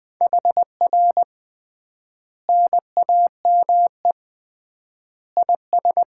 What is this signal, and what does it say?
Morse code, heard as a keyed tone, about 20 wpm, 700 Hz. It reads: HR NAME IS
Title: Common QSO Phrases - 3x Word-Spacing - 20wpm